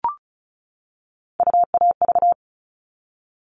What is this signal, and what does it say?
Morse code, heard as a keyed tone, about 34 wpm, 700 Hz. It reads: UAV
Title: All Cut Numbers - 3-Digit 35wpm